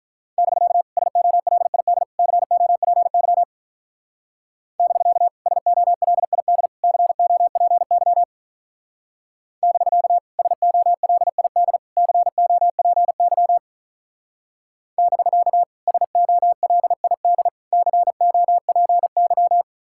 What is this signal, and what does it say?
Morse code, heard as a keyed tone, about 43 wpm, 700 Hz. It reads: <BK> SOLID COPY <BK> SOLID COPY <BK> SOLID COPY <BK> SOLID COPY
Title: Speed-Racing Common QSO Phrases Encore - 35wpm